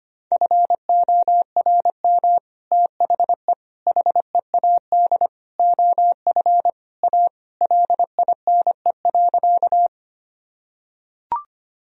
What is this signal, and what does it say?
Morse code, heard as a keyed tone, about 25 wpm, 700 Hz. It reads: FORM THE HEAD OF A LINE.
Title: Sentences from Top 1000 Words - Review Part 1 - 25wpm